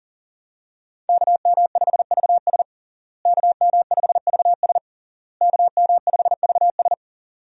Morse code, heard as a keyed tone, about 40 wpm, 700 Hz. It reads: KM5VS KM5VS KM5VS
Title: USA Call Signs - 5 Characters - Repeated 3x - 40wpm